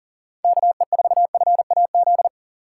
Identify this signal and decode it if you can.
Morse code, heard as a keyed tone, about 40 wpm, 700 Hz. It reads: KE4FAZ
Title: USA Call Signs - 2x Character Spacing - 40wpm